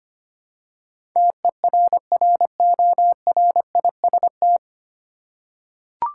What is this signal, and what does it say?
Morse code, heard as a keyed tone, about 25 wpm, 700 Hz. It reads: TERRORIST
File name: Nine Letter Words - 25wpm